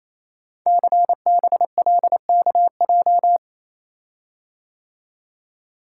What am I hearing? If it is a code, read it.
Morse code, heard as a keyed tone, about 28 wpm, 700 Hz. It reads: CBLKJ
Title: Code Groups - 5 Letters - 28wpm